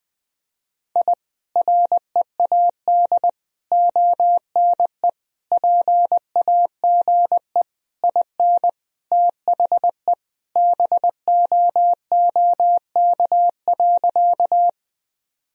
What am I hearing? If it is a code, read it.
Morse code, heard as a keyed tone, about 20 wpm, 700 Hz. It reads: I READ ONE PAGE IN THE BOOK.
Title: Sentences from Top 1000 Words - Review Part 1 - 20wpm